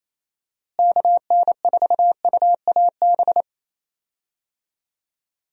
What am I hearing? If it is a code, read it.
Morse code, heard as a keyed tone, about 28 wpm, 700 Hz. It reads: KN4UAB
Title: USA Call Signs - 6 Characters - Repeated 2x - 28wpm